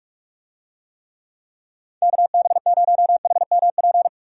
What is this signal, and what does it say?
Morse code, heard as a keyed tone, about 43 wpm, 700 Hz. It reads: KB0HMP
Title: USA Call Signs - 6 Characters - 45wpm